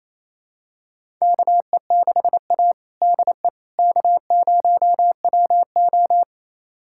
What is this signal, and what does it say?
Morse code, heard as a keyed tone, about 28 wpm, 700 Hz. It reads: KE6A DE K0WO
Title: USA Callsign DE Callsign - 4 Characters - 28wpm